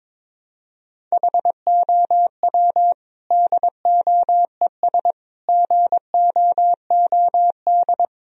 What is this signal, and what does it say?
Morse code, heard as a keyed tone, about 22 wpm, 700 Hz. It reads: HOW DOES GOOD
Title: Sentences from Top 1000 Words - Review Part 1 - 22wpm